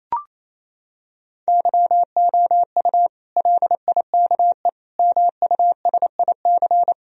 Morse code, heard as a keyed tone, about 28 wpm, 700 Hz. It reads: YOU LIKE MUSIC
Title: Sentences from Top 1000 Words - Review Part 1 - 28wpm